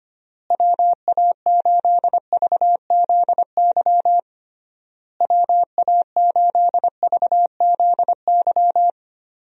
Morse code, heard as a keyed tone, about 25 wpm, 700 Hz. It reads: WA8VZY WA8VZY
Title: USA Call Signs - 6 Characters - Repeated 3x - 25wpm